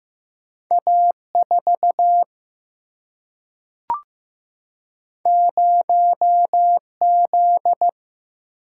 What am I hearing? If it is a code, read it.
Morse code, heard as a keyed tone, about 15 wpm, 700 Hz. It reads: A4 0Z
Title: Single-Letter Single-Number 15wpm